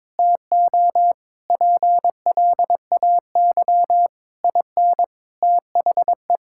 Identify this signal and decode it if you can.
Morse code, heard as a keyed tone, about 22 wpm, 700 Hz. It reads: TO PLAY IN THE
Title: Sentences from Top 1600 Words 22wpm